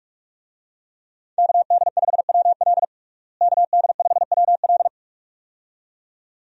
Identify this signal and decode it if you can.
Morse code, heard as a keyed tone, about 44 wpm, 700 Hz. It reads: KD5WL KD5WL
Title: USA Call Signs - 5 Characters - Repeated 2x - 45wpm